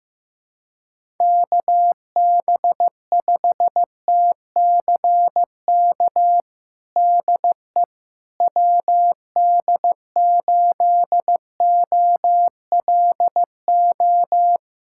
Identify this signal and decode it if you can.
Morse code, heard as a keyed tone, about 15 wpm, 700 Hz. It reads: KB5TCK DE WD8OLO
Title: USA Callsign DE Callsign - 6 Characters - 15wpm